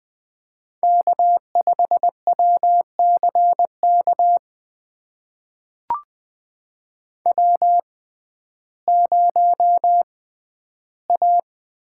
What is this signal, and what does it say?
Morse code, heard as a keyed tone, about 20 wpm, spaced as for 8 wpm, 700 Hz. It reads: K5WC KW0A
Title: USA Call Signs - 6x Character Spacing - 20wpm